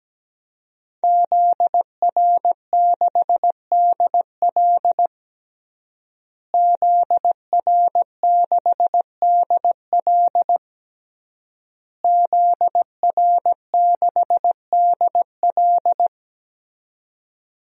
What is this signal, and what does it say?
Morse code, heard as a keyed tone, about 17 wpm, 700 Hz. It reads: ZR6DL ZR6DL ZR6DL
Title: DXCC-100 Call Signs - 5 Characters - Repeated 3x - 17wpm